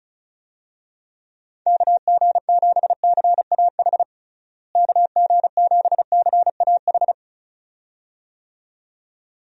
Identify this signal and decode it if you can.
Morse code, heard as a keyed tone, about 36 wpm, 700 Hz. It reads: KG7CAH KG7CAH
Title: USA Call Signs - 6 Characters - Repeated 2x - 35wpm